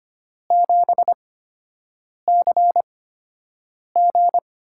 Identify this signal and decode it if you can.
Morse code, heard as a keyed tone, about 25 wpm, spaced as for 7 wpm, 700 Hz. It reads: 7CG
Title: USA Call Signs - 8x Character Spacing - 25wpm